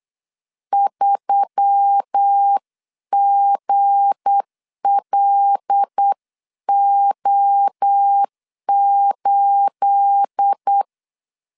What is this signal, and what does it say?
Morse code, heard as a keyed tone, about 8 wpm, 800 Hz. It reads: 3GLO8
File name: LEZIONI / 9 2867 8wpm 3440